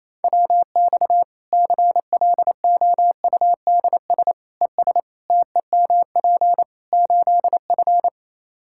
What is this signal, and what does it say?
Morse code, heard as a keyed tone, about 28 wpm, 700 Hz. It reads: WX CLOUDS ES TEMP 8F
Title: Practice QSO - Exchange 2 - Info - 28wpm